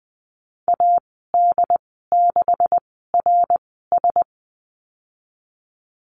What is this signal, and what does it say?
Morse code, heard as a keyed tone, about 20 wpm, spaced as for 15 wpm, 700 Hz. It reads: AD6RS
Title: 154 Lesson 25g - Letter p - Call Signs 20wpm